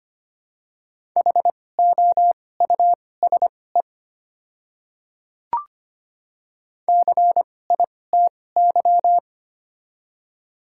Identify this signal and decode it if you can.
Morse code, heard as a keyed tone, about 25 wpm, spaced as for 18 wpm, 700 Hz. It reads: HOUSE CITY
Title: 2x Character Spacing - Top 300 Words - 25wpm